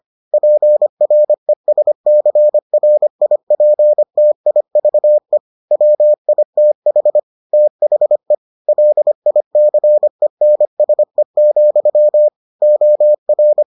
Morse code, heard as a keyed tone, about 25 wpm, 600 Hz. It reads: PRESCRIPTIVE WITH THE LICENSE, OR